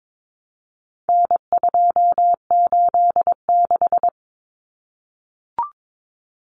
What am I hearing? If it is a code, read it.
Morse code, heard as a keyed tone, about 22 wpm, 700 Hz. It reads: N286
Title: Common Cut Numbers - 4-Digit 22wpm